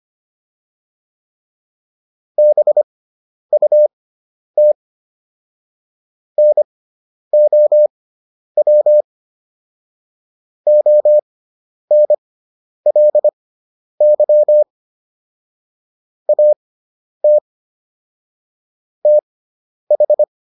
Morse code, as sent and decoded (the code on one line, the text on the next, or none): -... ..- - / -. --- .-- / --- -. .-.. -.-- / .- - / - ....
BUT NOW ONLY AT TH